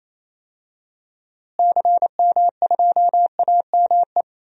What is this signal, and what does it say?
Morse code, heard as a keyed tone, about 28 wpm, 700 Hz. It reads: CM2AME
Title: DXCC-100 Call Signs - 6 Characters - Repeated 2x - 28wpm